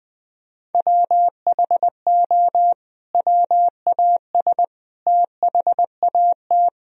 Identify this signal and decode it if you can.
Morse code, heard as a keyed tone, about 20 wpm, 700 Hz. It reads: WHO WAS THAT
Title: Sentences from Top 1000 Words - Review Part 4 - 20wpm